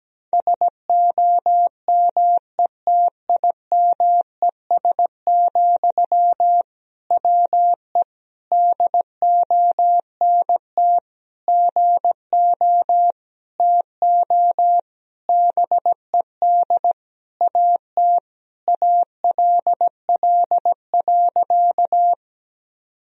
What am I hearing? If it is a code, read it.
Morse code, heard as a keyed tone, about 17 wpm, 700 Hz. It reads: SOMETIMES, WE DONT GO TO BED AT ALL.
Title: Sentences from Top 1000 Words - Review Part 3 - 17wpm